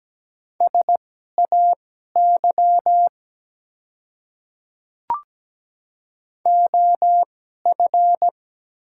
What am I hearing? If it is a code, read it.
Morse code, heard as a keyed tone, about 17 wpm, spaced as for 12 wpm, 700 Hz. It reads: SAY OF
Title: 2x Character Spacing - Top 200 Words - 17wpm